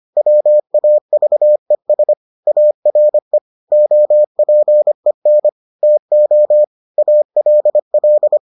Morse code, heard as a keyed tone, about 25 wpm, 600 Hz. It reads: WAVES ARE OPEN TO ALL